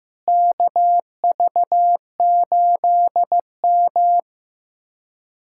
Morse code, heard as a keyed tone, about 15 wpm, 700 Hz. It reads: KV8M
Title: USA Callsign DE Callsign - 4 Characters - 15wpm